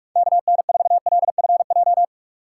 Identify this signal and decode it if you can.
Morse code, heard as a keyed tone, about 43 wpm, 700 Hz. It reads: KN4LFJ
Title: USA Call Signs - 2x Character Spacing - 45wpm